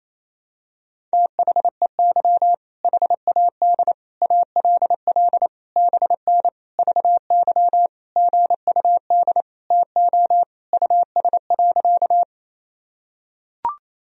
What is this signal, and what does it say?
Morse code, heard as a keyed tone, about 27 wpm, 700 Hz. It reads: THEY HAD ALL BN VY GUD TO US.
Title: Sentences from Top 200 Words and Abbreviations 28wpm